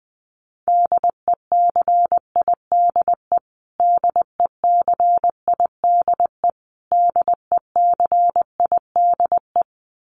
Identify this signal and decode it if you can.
Morse code, heard as a keyed tone, about 20 wpm, 700 Hz. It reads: DECIDE DECIDE DECIDE
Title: Top 500 Words Repeated 20wpm